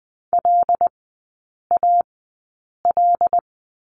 Morse code, heard as a keyed tone, about 20 wpm, spaced as for 8 wpm, 700 Hz. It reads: LAL
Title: 093 Lesson 16e - Letter c - Three Characters 20wpm